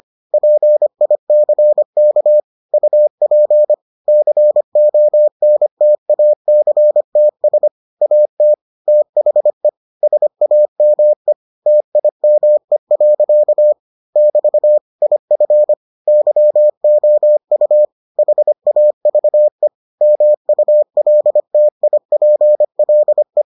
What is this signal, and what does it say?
Morse code, heard as a keyed tone, about 25 wpm, 600 Hz. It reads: PICK UP CONTACTS AT THE SAME TIME. <BT> IF YOU HAVE MULTIPLE